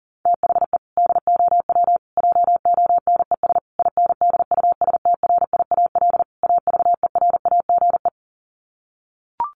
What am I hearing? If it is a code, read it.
Morse code, heard as a keyed tone, about 40 wpm, 700 Hz. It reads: THE DOW JONES INDUSTRIAL AVERAGE
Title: Sets of 5 Words 40wpm